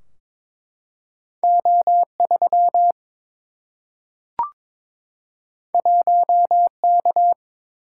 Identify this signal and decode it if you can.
Morse code, heard as a keyed tone, about 23 wpm, 700 Hz. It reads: O 3 1 K
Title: Single-Letter Single-Number 22wpm